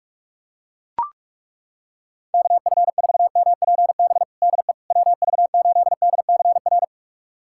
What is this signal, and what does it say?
Morse code, heard as a keyed tone, about 41 wpm, 700 Hz. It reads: KF4MPB DE WV8DCR
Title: USA Callsign DE Callsign 45wpm